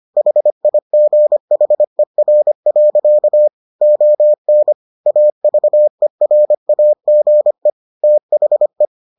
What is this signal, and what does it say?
Morse code, heard as a keyed tone, about 25 wpm, 600 Hz. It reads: HIGHER. ON AVERAGE THE